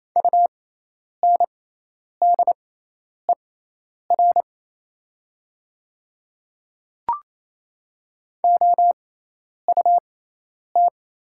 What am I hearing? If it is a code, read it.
Morse code, heard as a keyed tone, about 28 wpm, spaced as for 10 wpm, 700 Hz. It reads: UNDER OUT
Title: 6x Character Spacing - Top 200 Words - 28wpm